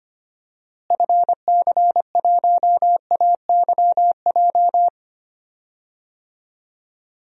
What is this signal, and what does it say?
Morse code, heard as a keyed tone, about 25 wpm, 700 Hz. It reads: FC1AYJ
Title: DXCC-100 Call Signs - 6 Characters - Repeated 3x - 25wpm